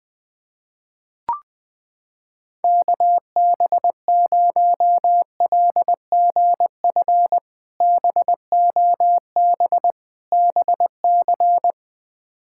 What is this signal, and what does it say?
Morse code, heard as a keyed tone, about 20 wpm, 700 Hz. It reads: KB0LGF BOB BC
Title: K1USN SST QSO Practice - 20wpm